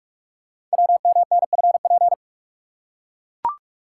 Morse code, heard as a keyed tone, about 44 wpm, 700 Hz. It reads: WMNFP
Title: Code Groups - 5 Letters - 45wpm